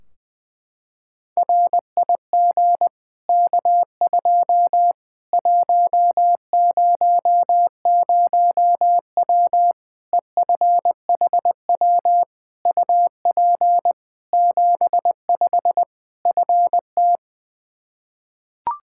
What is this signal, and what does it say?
Morse code, heard as a keyed tone, about 20 wpm, 700 Hz. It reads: RIG K2 100W EFHW UP 75 FT
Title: Practice QSO - Exchange 2 - Info - 20wpm